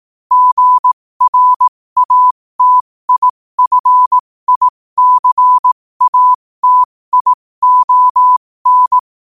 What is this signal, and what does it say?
Morse code, heard as a keyed tone, about 18 wpm, spaced as for 15 wpm, 1000 Hz. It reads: GRATIFICATION
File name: qotdfiles / 15wpm